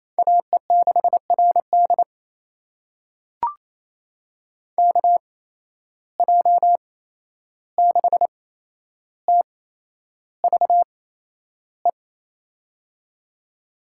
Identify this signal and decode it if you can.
Morse code, heard as a keyed tone, about 27 wpm, spaced as for 11 wpm, 700 Hz. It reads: AE6RD KJ6TVE
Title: USA Call Signs - 8x Character Spacing - 28wpm